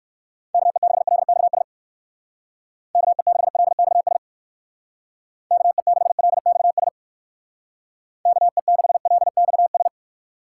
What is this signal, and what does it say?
Morse code, heard as a keyed tone, about 63 wpm, 700 Hz. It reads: KE6LXS KE6LXS KE6LXS KE6LXS